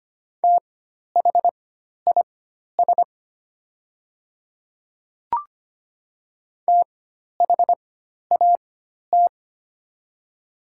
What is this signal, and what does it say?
Morse code, heard as a keyed tone, about 25 wpm, spaced as for 12 wpm, 700 Hz. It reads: THIS THAT
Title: 4x Character Spacing - Top 100 Words - 25wpm